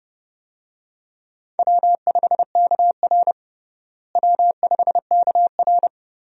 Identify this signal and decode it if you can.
Morse code, heard as a keyed tone, about 30 wpm, 700 Hz. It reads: W5KR W5KR
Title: USA Call Signs - 4 Characters - Repeated 2x - 30wpm